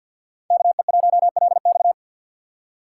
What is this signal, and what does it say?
Morse code, heard as a keyed tone, about 51 wpm, 690 Hz. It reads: KE1LX
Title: USA Call Signs - 5 Characters - 3x Character Spacing - 50wpm